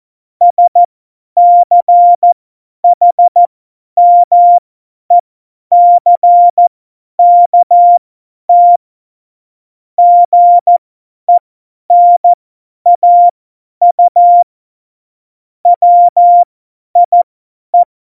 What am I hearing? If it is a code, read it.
Morse code, heard as a keyed tone, about 14 wpm, spaced as for 10 wpm, 700 Hz. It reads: SCHMECKT GENAU WIE
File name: Morsekurs / Witze / Witz 11. Februar 2020, 50 F70